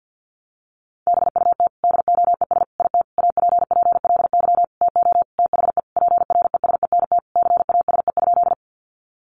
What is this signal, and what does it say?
Morse code, heard as a keyed tone, about 50 wpm, 700 Hz. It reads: BUT DOES IT APPLY TO THE PRESENT CASE?
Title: Sentences from Top 1200 Words 50wpm